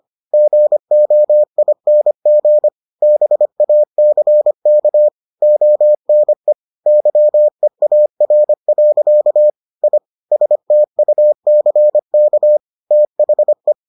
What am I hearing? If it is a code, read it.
Morse code, heard as a keyed tone, about 25 wpm, 600 Hz. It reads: GOING BACK ONE YEAR. I STUCK THE